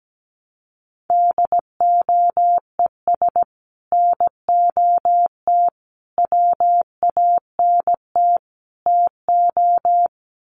Morse code, heard as a keyed tone, about 17 wpm, 700 Hz. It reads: DOES NOT WANT TO
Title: Sets of 4 Words 17wpm